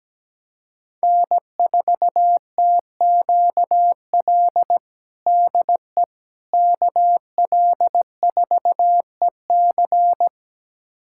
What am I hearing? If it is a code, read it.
Morse code, heard as a keyed tone, about 17 wpm, 700 Hz. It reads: N4TQL DE KL4EC
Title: USA Callsign DE Callsign - 5 Characters - 17wpm